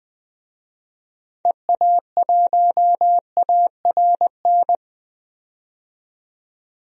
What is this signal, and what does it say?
Morse code, heard as a keyed tone, about 21 wpm, 700 Hz. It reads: EA1ARN
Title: DXCC-100 Call Signs - 6 Characters - 20wpm